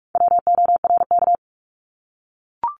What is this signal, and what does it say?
Morse code, heard as a keyed tone, about 44 wpm, 700 Hz. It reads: WORK
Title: ED Suffix - Top 100 Words with Base Words 45wpm